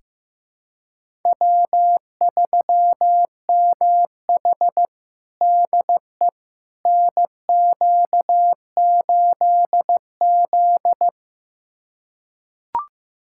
Morse code, heard as a keyed tone, about 15 wpm, 700 Hz. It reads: W3MH DE NQ8Z
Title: USA Callsign DE Callsign - 4 Characters - 15wpm